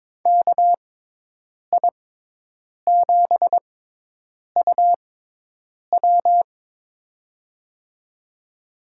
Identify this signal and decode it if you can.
Morse code, heard as a keyed tone, about 22 wpm, spaced as for 8 wpm, 700 Hz. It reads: KI7UW
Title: USA Call Signs - 5 Characters - 6x Character Spacing - 22wpm